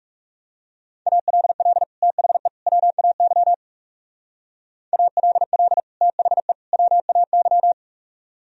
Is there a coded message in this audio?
Yes: Morse code from 1.1 to 7.7 s, heard as a keyed tone, about 39 wpm, 700 Hz. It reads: ALL THE WAY ALL THE WAY